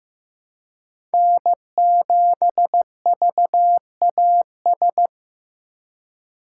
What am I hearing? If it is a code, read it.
Morse code, heard as a keyed tone, about 15 wpm, 700 Hz. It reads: N7VAS
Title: USA Call Signs - 5 Characters - 2x Character Spacing - 15wpm